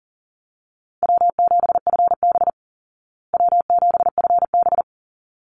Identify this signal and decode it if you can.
Morse code, heard as a keyed tone, about 40 wpm, 700 Hz. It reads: W7FB W7FB
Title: Call Signs Repeated 3x - 40wpm